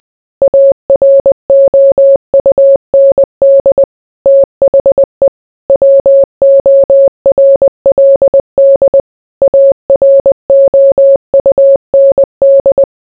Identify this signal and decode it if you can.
Morse code, heard as a keyed tone, about 20 wpm, 550 Hz. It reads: AROUND THE WORLD AROUND